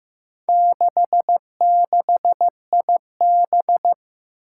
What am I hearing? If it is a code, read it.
Morse code, heard as a keyed tone, about 16 wpm, 700 Hz. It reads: 66IB
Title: Four Challenging Letter-Number Combinations 15wpm